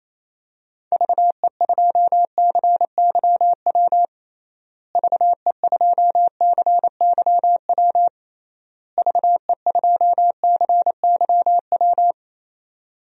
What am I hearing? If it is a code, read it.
Morse code, heard as a keyed tone, about 27 wpm, 700 Hz. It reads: VE2CYW VE2CYW VE2CYW
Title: DXCC-100 Call Signs - 6 Characters - Repeated 3x - 28wpm